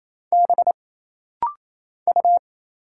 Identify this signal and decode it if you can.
Morse code, heard as a keyed tone, about 28 wpm, 700 Hz. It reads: B U
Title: Single Letter-Number Alt - Rapid-Fire 28wpm